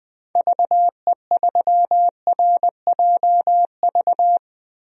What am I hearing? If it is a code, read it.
Morse code, heard as a keyed tone, about 20 wpm, 700 Hz. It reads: VE3RJV
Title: DXCC-100 Call Signs - 6 Characters - Repeated 2x - 20wpm